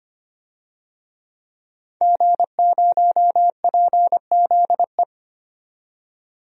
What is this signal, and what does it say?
Morse code, heard as a keyed tone, about 25 wpm, 700 Hz. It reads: G0PZE
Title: DXCC-100 Call Signs - 5 Characters - 25wpm